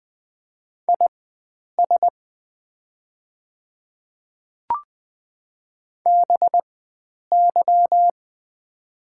Morse code, heard as a keyed tone, about 20 wpm, spaced as for 9 wpm, 700 Hz. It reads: IS BY
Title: 4x Character Spacing - Top 100 Words - 20wpm